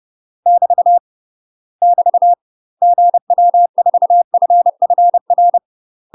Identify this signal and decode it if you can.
Morse code, heard as a keyed tone, about 30 wpm, 700 Hz. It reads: <BT> <BT> GW4FFR
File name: UK MASTER QSOs TRAINING / 30WPM / UK MASTER QSOs 0023